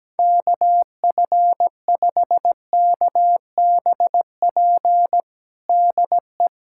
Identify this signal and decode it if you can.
Morse code, heard as a keyed tone, about 17 wpm, 700 Hz. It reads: KF5KBP DE
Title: USA Callsign DE Callsign 17wpm